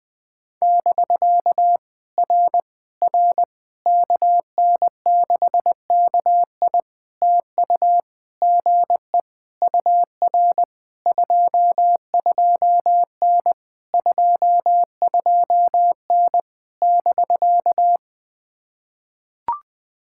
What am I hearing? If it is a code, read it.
Morse code, heard as a keyed tone, about 20 wpm, 700 Hz. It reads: <BK> R R KN6KI TU GE UR 22N 22N <BK>
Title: POTA QSO Practice - 20wpm